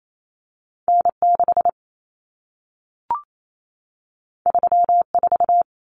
Morse code, heard as a keyed tone, about 28 wpm, 700 Hz. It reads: N6 34
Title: Common Cut Numbers - 2-Digit 28wpm